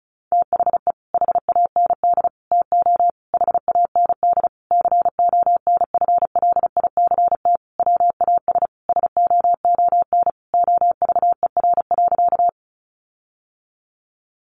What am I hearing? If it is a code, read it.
Morse code, heard as a keyed tone, about 34 wpm, 700 Hz. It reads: THE HAND TO HAND CONFLICT WAS SOON OVER.
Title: Sentences from Top 1300 Words 35wpm